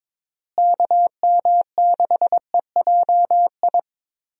Morse code, heard as a keyed tone, about 21 wpm, 700 Hz. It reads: KM6EJI
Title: USA Call Signs - 6 Characters - Repeated 3x - 22wpm